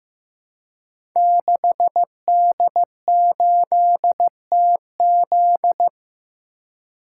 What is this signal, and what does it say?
Morse code, heard as a keyed tone, about 15 wpm, 700 Hz. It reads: 6D8TZ
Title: Code Groups - 5 Letter:Numbers - 15wpm